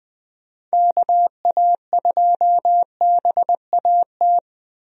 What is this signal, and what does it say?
Morse code, heard as a keyed tone, about 20 wpm, 700 Hz. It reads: KA2BAT
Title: USA Call Signs - 6 Characters - 20wpm